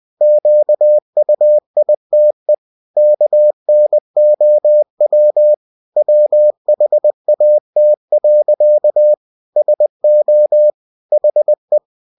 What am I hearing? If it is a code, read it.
Morse code, heard as a keyed tone, about 20 wpm, 600 Hz. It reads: QUITE KNOW WHAT. SO HE